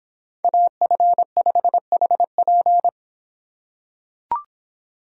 Morse code, heard as a keyed tone, about 26 wpm, 700 Hz. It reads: AF5HP